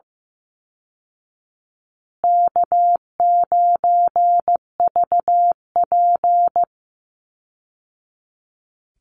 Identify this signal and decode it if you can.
Morse code, heard as a keyed tone, about 15 wpm, 700 Hz. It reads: K9VP
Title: Easy Call Signs Repeated 2x - 15wpm